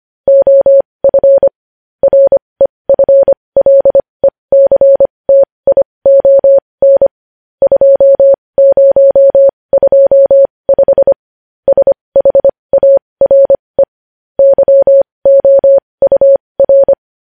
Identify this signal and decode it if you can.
Morse code, heard as a keyed tone, about 25 wpm, spaced as for 20 wpm, 550 Hz. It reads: OF REFLECTION 2025 SHARE YOUR
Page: news-headlines-in-morse-code-at-20-wpm